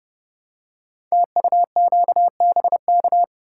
Morse code, heard as a keyed tone, about 30 wpm, 700 Hz. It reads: TUQBK
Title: Code Groups - 5 Letters - 30wpm